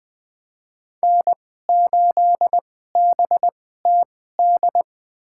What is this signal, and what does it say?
Morse code, heard as a keyed tone, about 20 wpm, spaced as for 14 wpm, 700 Hz. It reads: N8BTD
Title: USA Call Signs - 2x Character Spacing - 20wpm